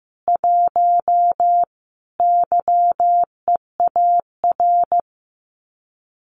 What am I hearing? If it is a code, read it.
Morse code, heard as a keyed tone, about 15 wpm, 700 Hz. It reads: 1 YEAR
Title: Common QSO Phrases Encore 15wpm